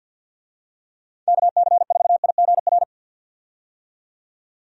Morse code, heard as a keyed tone, about 50 wpm, 700 Hz. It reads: KC4IGR
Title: USA Call Signs - 6 Characters - 50wpm